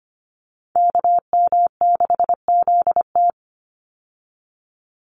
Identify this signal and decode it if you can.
Morse code, heard as a keyed tone, about 25 wpm, 700 Hz. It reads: KM6ZT
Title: Call Signs - 25wpm